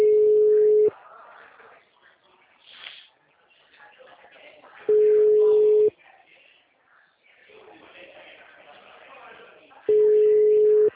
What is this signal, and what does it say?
Ringback tone (Europe, 425 Hz), with speech.
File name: out- -2016- - - .155